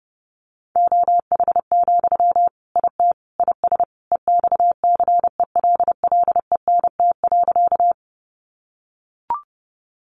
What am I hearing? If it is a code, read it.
Morse code, heard as a keyed tone, about 30 wpm, 700 Hz. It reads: OH, IT IS EXCELLENT.
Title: Sentences from Top 1000 Words 30wpm